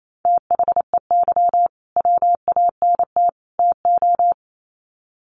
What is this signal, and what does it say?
Morse code, heard as a keyed tone, about 28 wpm, 700 Hz. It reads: THEY WANT TO
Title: Sets of 3 Words 28wpm